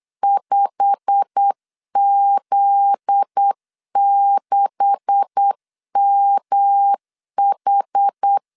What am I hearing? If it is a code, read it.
Morse code, heard as a keyed tone, about 8 wpm, 800 Hz. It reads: 5Z6MH
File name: CW / DET / 1 9 1500C S C 8wpm 5653